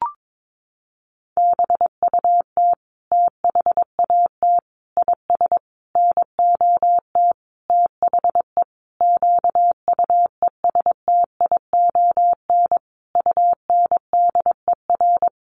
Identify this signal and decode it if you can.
Morse code, heard as a keyed tone, about 22 wpm, 700 Hz. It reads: BUT THAT IS NOT THE QUESTION UNDER
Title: Sentences from Top 1500 Words 22wpm